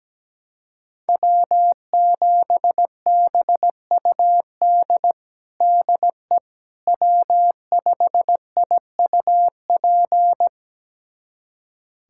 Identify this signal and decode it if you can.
Morse code, heard as a keyed tone, about 17 wpm, 700 Hz. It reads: W7BUD DE W5IUP
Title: USA Callsign DE Callsign - 5 Characters - 17wpm